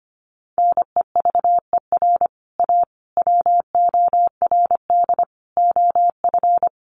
Morse code, heard as a keyed tone, about 25 wpm, 700 Hz. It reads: NEVER A WORD OF
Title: Sentences from Top 1300 Words 25wpm